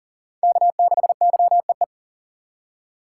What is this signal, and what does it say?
Morse code, heard as a keyed tone, about 40 wpm, 700 Hz. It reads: K6YEE
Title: USA Call Signs - 5 Characters - 2x Character Spacing - 40wpm